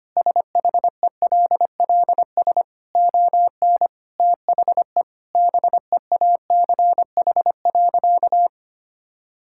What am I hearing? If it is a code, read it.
Morse code, heard as a keyed tone, about 25 wpm, 700 Hz. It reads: SHELLS ON THE BEACH.
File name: Easy Sentences with Hints - 25wpm